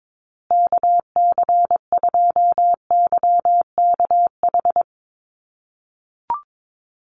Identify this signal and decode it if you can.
Morse code, heard as a keyed tone, about 22 wpm, 700 Hz. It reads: KC2YKH
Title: USA Call Signs Repeated 2x 22wpm